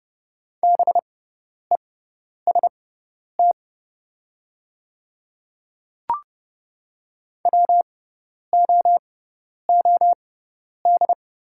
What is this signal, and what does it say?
Morse code, heard as a keyed tone, about 30 wpm, spaced as for 10 wpm, 700 Hz. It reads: BEST WOOD
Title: 6x Character Spacing - Top 500 Words - 30wpm